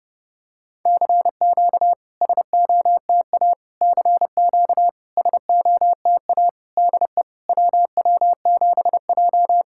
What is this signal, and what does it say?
Morse code, heard as a keyed tone, about 30 wpm, 700 Hz. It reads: CQ SOTA CQ SOTA DE WW7J
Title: SOTA QSO Practice 30wpm